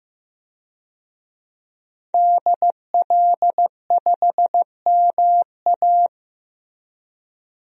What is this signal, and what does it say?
Morse code, heard as a keyed tone, about 15 wpm, 700 Hz. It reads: DL5MA
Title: DXCC-100 Call Signs - 5 Characters - Repeated 3x - 15wpm